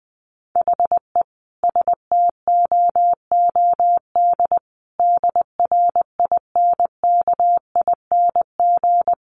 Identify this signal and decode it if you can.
Morse code, heard as a keyed tone, about 20 wpm, 700 Hz. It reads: HE STOOD DRINKING
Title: A Sound Of Thunder 20wpm